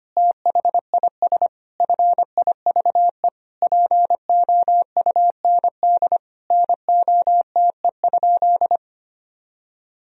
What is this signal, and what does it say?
Morse code, heard as a keyed tone, about 25 wpm, 700 Hz. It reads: THIS FIVE POUND NOTE?
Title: Sentences from Top 1000 Words - Review Part 2 - 25wpm